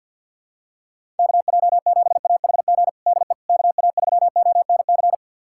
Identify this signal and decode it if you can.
Morse code, heard as a keyed tone, about 51 wpm, 700 Hz. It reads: KJ7AHG DE KA3ONC
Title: USA Callsign DE Callsign 50wpm